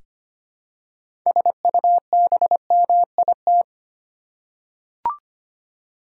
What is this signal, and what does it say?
Morse code, heard as a keyed tone, about 25 wpm, 700 Hz. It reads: SUBMIT
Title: Six Letter Words - 25wpm